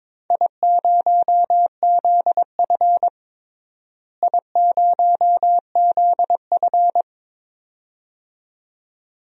Morse code, heard as a keyed tone, about 22 wpm, 700 Hz. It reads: I0ZF I0ZF
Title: DXCC-100 Call Signs - 4 Characters - Repeated 2x - 22wpm